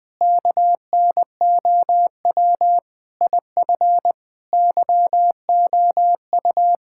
Morse code, heard as a keyed tone, about 21 wpm, 700 Hz. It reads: KNOW IF YOU
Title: Sentences with 5-Word Phrases - 20wpm